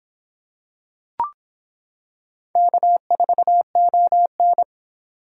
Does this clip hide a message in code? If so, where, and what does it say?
Yes: Morse code from 2.6 to 4.6 s, heard as a keyed tone, about 25 wpm, 700 Hz. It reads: K4ON